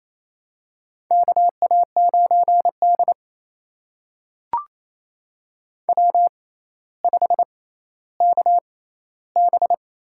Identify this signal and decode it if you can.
Morse code, heard as a keyed tone, about 27 wpm, spaced as for 14 wpm, 700 Hz. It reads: KA9D W5KB
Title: USA Call Signs - 4 Characters - 6x Character Spacing - 28wpm